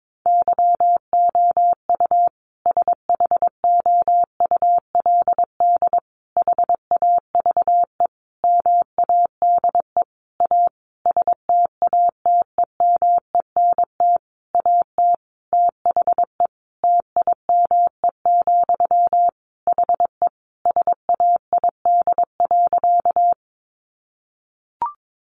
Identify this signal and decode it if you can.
Morse code, heard as a keyed tone, about 22 wpm, 700 Hz. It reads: YOU SHOULD HAVE MADE A STATEMENT AT THE TIME, HE SAID.
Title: Sentences from Top 1200 Words 22wpm